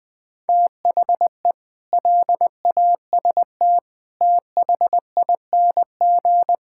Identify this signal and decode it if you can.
Morse code, heard as a keyed tone, about 21 wpm, 700 Hz. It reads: THE LAST THING
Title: Sets of 8 Words - 20wpm